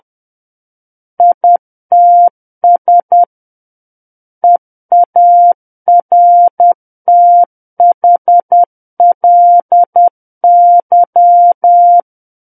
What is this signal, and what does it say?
Morse code, heard as a keyed tone, about 10 wpm, 700 Hz. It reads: ITS EARTHLY
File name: Book One / 01-Chapter 1 - 10 WPM